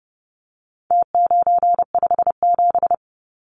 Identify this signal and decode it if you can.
Morse code, heard as a keyed tone, about 30 wpm, 700 Hz. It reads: T957
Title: Numbers - Contest Style - 4-Digit 30wpm